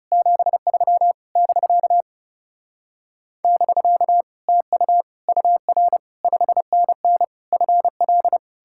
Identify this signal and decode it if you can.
Morse code, heard as a keyed tone, about 40 wpm, 700 Hz. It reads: 73 <BK> <BK> TU UR 5NN FL